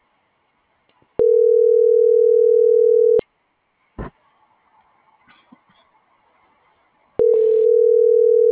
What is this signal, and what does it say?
Ringback tone (North America), 440 + 480 Hz.